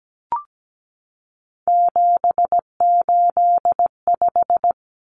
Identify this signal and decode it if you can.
Morse code, heard as a keyed tone, about 17 wpm, 700 Hz. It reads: 785
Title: Numbers - 3-Digit 17wpm